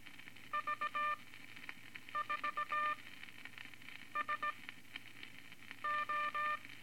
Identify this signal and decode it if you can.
Morse code, heard as a keyed tone, about 18 wpm, spaced as for 6 wpm, 1200 Hz, with noise. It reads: V4SO
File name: Morse Code Records / 13A LCM 154 Units I-IV Groups of Four cont